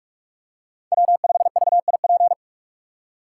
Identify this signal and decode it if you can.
Morse code, heard as a keyed tone, about 42 wpm, 700 Hz. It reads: W5VIP